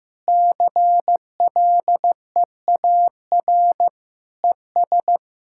Morse code, heard as a keyed tone, about 15 wpm, 700 Hz. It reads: CLEAR ES
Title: K1USN SST QSO Practice 15wpm